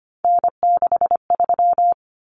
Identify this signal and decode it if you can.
Morse code, heard as a keyed tone, about 25 wpm, 700 Hz. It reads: N63
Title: Common Cut Numbers - 3-Digit 25wpm